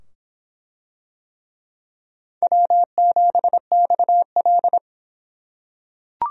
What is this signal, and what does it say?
Morse code, heard as a keyed tone, about 25 wpm, 700 Hz. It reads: W7XL